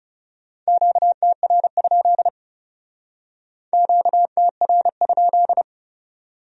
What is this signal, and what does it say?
Morse code, heard as a keyed tone, about 37 wpm, 700 Hz. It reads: QTR? QTR?
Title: Speed-Racing Q-Codes - 30wpm